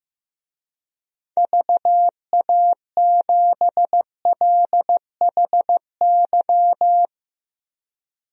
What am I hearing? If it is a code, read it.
Morse code, heard as a keyed tone, about 16 wpm, 700 Hz. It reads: VA7LHY
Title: DXCC-100 Call Signs - 6 Characters - Repeated 3x - 15wpm